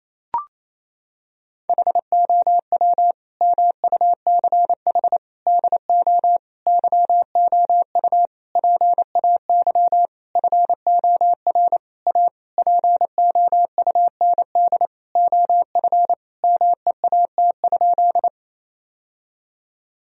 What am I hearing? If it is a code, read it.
Morse code, heard as a keyed tone, about 28 wpm, 700 Hz. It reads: HOW MUCH DO YOU PAY FOR A POUND OF MEAT?
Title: Sentences from Top 1000 Words - Review Part 3 - 28wpm